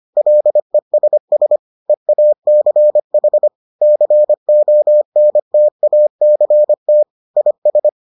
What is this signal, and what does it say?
Morse code, heard as a keyed tone, about 25 wpm, 600 Hz. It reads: LESS EACH CONTACT IS